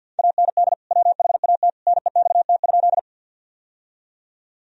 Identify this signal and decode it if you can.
Morse code, heard as a keyed tone, about 52 wpm, 700 Hz. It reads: AND WHAT NEXT?
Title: Sentences from Top 1000 Words - Review Part 1 - 50wpm